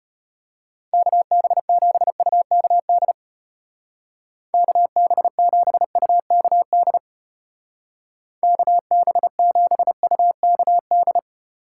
Morse code, heard as a keyed tone, about 31 wpm, 700 Hz. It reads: KB7UKD KB7UKD KB7UKD